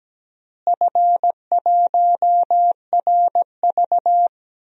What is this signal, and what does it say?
Morse code, heard as a keyed tone, about 17 wpm, 700 Hz. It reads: F1RV
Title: DXCC-100 Call Signs - 4 Characters - Repeated 3x - 17wpm